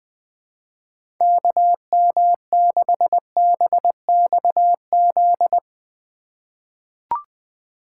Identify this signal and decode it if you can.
Morse code, heard as a keyed tone, about 20 wpm, 700 Hz. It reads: KM6BXZ
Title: USA Call Signs - 6 Characters - 2x Character Spacing - 20wpm